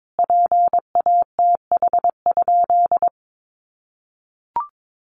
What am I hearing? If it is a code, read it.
Morse code, heard as a keyed tone, about 22 wpm, 700 Hz. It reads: PATH?
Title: Sentences from Top 1400 Words 22wpm